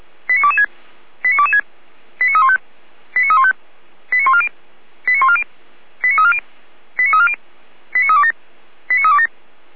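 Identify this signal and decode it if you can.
Five-tone selective call (ZVEI), heard as quick runs of five tones, tones of about 70 ms. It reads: 89226 89226 89315 89315 89140 89140 89340 89340 89327 89327
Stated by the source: ZVEI five-tone call 89246 received on SDR.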